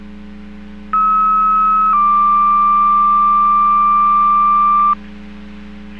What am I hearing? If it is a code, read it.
Two-tone page, heard as a one-second tone, then a three-second tone, about 1290 then 1180 Hz. It is heard from 0.9 to 4.9 s.